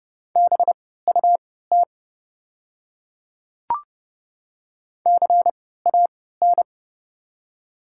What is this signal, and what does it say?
Morse code, heard as a keyed tone, about 30 wpm, spaced as for 17 wpm, 700 Hz. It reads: BUT CAN
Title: 3x Character Spacing - Top 100 Words - 30wpm